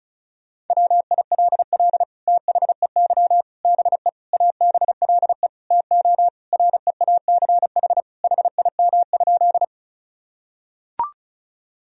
Morse code, heard as a keyed tone, about 36 wpm, 700 Hz. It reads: WILL THEY BE ABLE TO REACH HIM?
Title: Sentences from Top 1000 Words - Review Part 2 - 35wpm